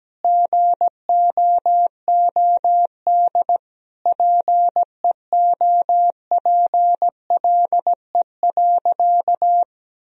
Morse code, heard as a keyed tone, about 17 wpm, 700 Hz. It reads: GOOD PEOPLE.
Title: Sentences from Top 1000 Words - Review Part 2 - 17wpm